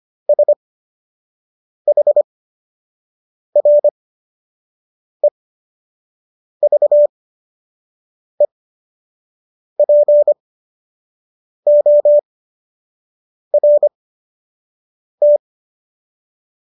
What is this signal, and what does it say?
Morse code, heard as a keyed tone, about 25 wpm, spaced as for 6 wpm, 590 Hz. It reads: SHREVEPORT